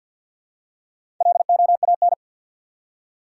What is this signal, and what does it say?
Morse code, heard as a keyed tone, about 52 wpm, 700 Hz. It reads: LOAN
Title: Four Letter Words - 50wpm